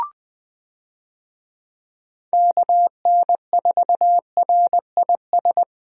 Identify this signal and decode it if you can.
Morse code, heard as a keyed tone, about 21 wpm, 700 Hz. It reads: KN4RIS
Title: USA Call Signs - 6 Characters - Repeated 3x - 20wpm